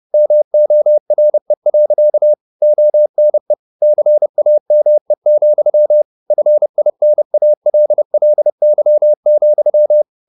Morse code, heard as a keyed tone, about 30 wpm, 600 Hz. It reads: MORE. ONE CAME, FINALLY,